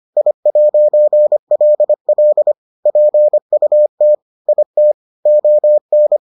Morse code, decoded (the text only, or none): I'LL PUT IT ON